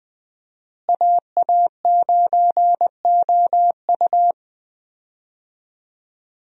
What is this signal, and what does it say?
Morse code, heard as a keyed tone, about 20 wpm, 700 Hz. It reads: AA9OU
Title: USA Call Signs - 5 Characters - Repeated 3x - 20wpm